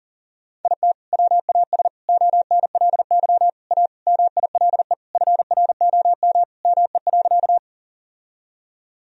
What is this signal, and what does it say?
Morse code, heard as a keyed tone, about 40 wpm, 700 Hz. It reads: IT WAS ONLY A MILE FROM ME.
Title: Sentences from Top 1000 Words - Review Part 1 - 40wpm